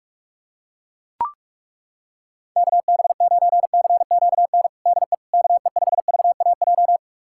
Morse code, heard as a keyed tone, about 41 wpm, 700 Hz. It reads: KB9CQN DE KE5VAJ
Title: USA Callsign DE Callsign - 6 Characters - 45wpm